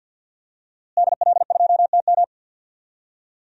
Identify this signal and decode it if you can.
Morse code, heard as a keyed tone, about 51 wpm, 700 Hz. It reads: DL2TM
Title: DXCC-100 Call Signs - 5 Characters - Repeated 3x - 50wpm